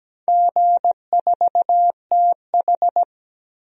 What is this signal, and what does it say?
Morse code, heard as a keyed tone, about 17 wpm, 700 Hz. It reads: G4TH
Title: DXCC-100 Call Signs - 4 Characters - 17wpm